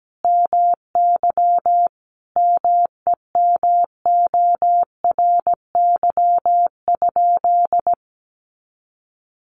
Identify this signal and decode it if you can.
Morse code, heard as a keyed tone, about 17 wpm, 700 Hz. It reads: MY MEMORY?
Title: Sentences from Top 1300 Words 17wpm